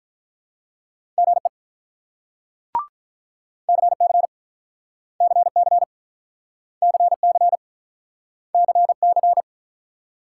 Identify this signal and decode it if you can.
Morse code, heard as a keyed tone, about 46 wpm, 700 Hz. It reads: GE CC CC CC CC